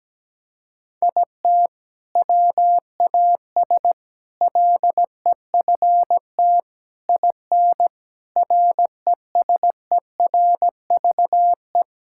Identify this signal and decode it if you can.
Morse code, heard as a keyed tone, about 17 wpm, 700 Hz. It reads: IT WAS LEFT IN RESERVE
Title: Sentences from Top 1000 Words - Review Part 6 - 17wpm